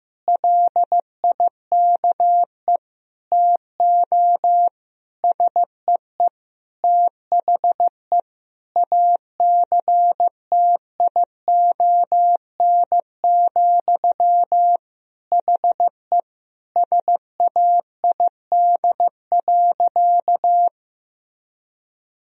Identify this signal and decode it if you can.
Morse code, heard as a keyed tone, about 15 wpm, 700 Hz. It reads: LIKE TO SEE THE ACTION, HE SAID.
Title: Sentences from Top 1000 Words - Review Part 4 - 15wpm